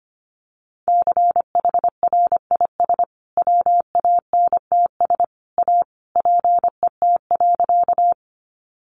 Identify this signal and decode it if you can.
Morse code, heard as a keyed tone, about 25 wpm, 700 Hz. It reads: CHRIS WANTS A PET.
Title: Additional Easy Phrases 25wpm